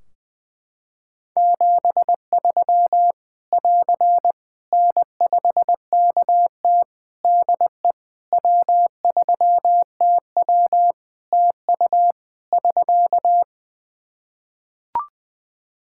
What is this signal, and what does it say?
Morse code, heard as a keyed tone, about 20 wpm, 700 Hz. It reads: 73 <AR> N5KT DE W3TW TU <SK>
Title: Practice QSO - Exchange 4 - Ending - 20wpm